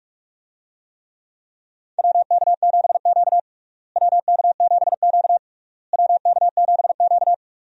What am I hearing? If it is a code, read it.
Morse code, heard as a keyed tone, about 47 wpm, 700 Hz. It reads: WK7Q WK7Q WK7Q
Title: USA Call Signs - 4 Characters - Repeated 3x - 45wpm